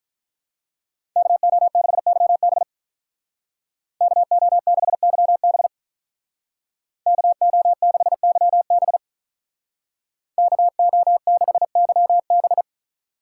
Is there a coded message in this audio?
Yes: Morse code from 1.2 to 12.6 s, heard as a keyed tone, about 46 wpm, 700 Hz. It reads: KO6YB KO6YB KO6YB KO6YB